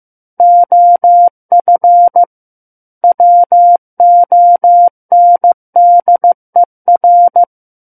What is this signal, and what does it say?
Morse code, heard as a keyed tone, about 15 wpm, 700 Hz. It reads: OF WONDER
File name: Book One / 06-Chapter 6 - 15 WPM